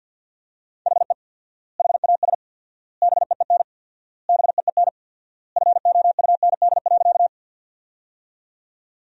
Morse code, heard as a keyed tone, about 50 wpm, 700 Hz. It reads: HE HAS BEEN BEEN FOUND.
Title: 4x Word-Spacing - Sentences from Top 300 Words - 50wpm